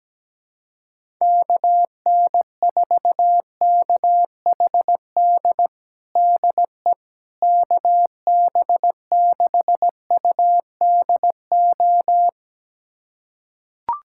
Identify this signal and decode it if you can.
Morse code, heard as a keyed tone, about 17 wpm, 700 Hz. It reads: KN4KHD DE KB6UDO
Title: USA Callsign DE Callsign 17wpm